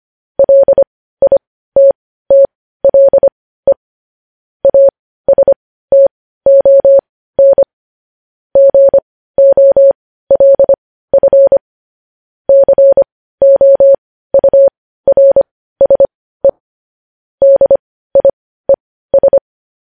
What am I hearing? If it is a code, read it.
Morse code, heard as a keyed tone, about 25 wpm, spaced as for 15 wpm, 550 Hz. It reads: LITTLE ASTON GOLF COURSE DIES
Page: news-headlines-in-morse-code-at-15-wpm